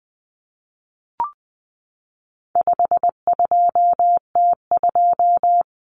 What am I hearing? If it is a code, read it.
Morse code, heard as a keyed tone, about 20 wpm, 700 Hz. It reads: 52T2
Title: Common Cut Numbers - 4-Digit 20wpm